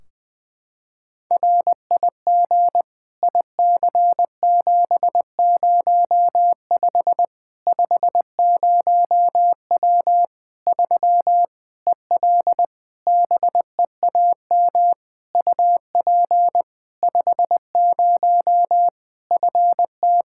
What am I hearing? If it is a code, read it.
Morse code, heard as a keyed tone, about 20 wpm, 700 Hz. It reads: RIG IC705 50W 3 EL BEAM UP 50 FT
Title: Practice QSO - Exchange 2 - Info - 20wpm